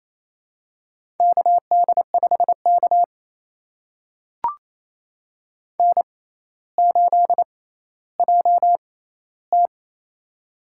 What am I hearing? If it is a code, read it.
Morse code, heard as a keyed tone, about 27 wpm, spaced as for 14 wpm, 700 Hz. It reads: KD5K N8JT
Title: USA Call Signs - 4 Characters - 6x Character Spacing - 28wpm